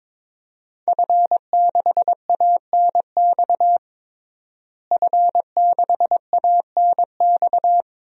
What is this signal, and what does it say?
Morse code, heard as a keyed tone, about 22 wpm, 700 Hz. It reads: F6ANX F6ANX
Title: DXCC-100 Call Signs - 5 Characters - Repeated 2x - 22wpm